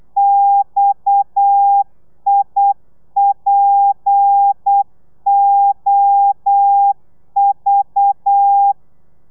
Morse code, heard as a keyed tone, about 8 wpm, 800 Hz. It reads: XIPOV